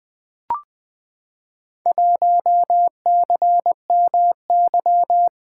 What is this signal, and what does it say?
Morse code, heard as a keyed tone, about 20 wpm, 700 Hz. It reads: 1CMY
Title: Four Letter-Number Combinations - 20wpm